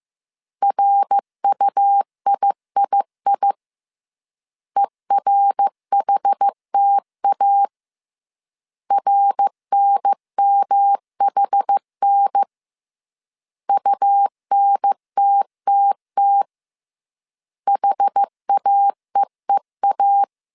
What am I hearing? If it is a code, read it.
Morse code, heard as a keyed tone, about 15 wpm, 800 Hz. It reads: RUIII ERHTA RNMHN UNTTT HAEEA